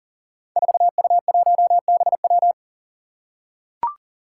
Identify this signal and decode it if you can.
Morse code, heard as a keyed tone, about 40 wpm, 700 Hz. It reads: 4U1BW
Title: Five Challenging Letter-Number Combinations 40wpm